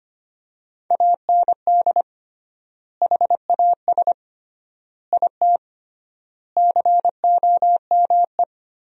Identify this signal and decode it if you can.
Morse code, heard as a keyed tone, about 25 wpm, 700 Hz. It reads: AND HAS IT COME
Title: 3x Word-Spacing - Sentences from Top 100 Words - 25wpm